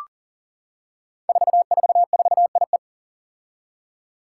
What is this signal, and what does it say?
Morse code, heard as a keyed tone, about 40 wpm, 700 Hz. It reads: 444IE
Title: Five Challenging Letter-Number Combinations 40wpm